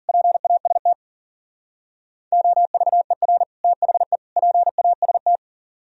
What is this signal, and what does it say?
Morse code, heard as a keyed tone, about 49 wpm, 700 Hz. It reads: PAST OVER THE PAST
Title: Speed-Racing - Sets of 3 Words 40wpm